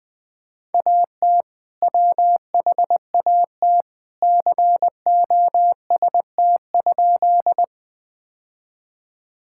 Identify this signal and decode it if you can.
Morse code, heard as a keyed tone, about 20 wpm, 700 Hz. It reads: AT WHAT COST?
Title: Sentences from Top 1000 Words - Review Part 5 - 20wpm